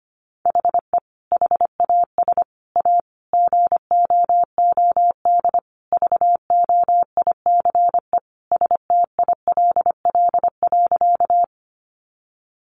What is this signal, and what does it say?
Morse code, heard as a keyed tone, about 25 wpm, 700 Hz. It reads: HE HAS A GOOD VOICE STILL.
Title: Sentences from Top 500 Words 25wpm